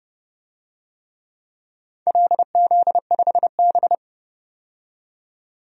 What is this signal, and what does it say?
Morse code, heard as a keyed tone, about 30 wpm, 700 Hz. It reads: LZ5B
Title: DXCC-100 Call Signs - 4 Characters - 30wpm